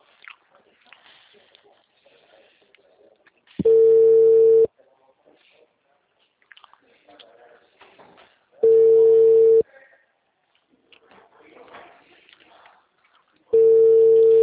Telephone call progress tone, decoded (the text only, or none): ringback tone